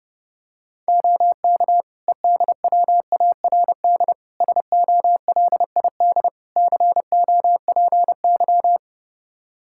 Morse code, heard as a keyed tone, about 31 wpm, 700 Hz. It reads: OK EDWARD SOLID COPY
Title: Practice QSO - Exchange 3 - More Info - 30wpm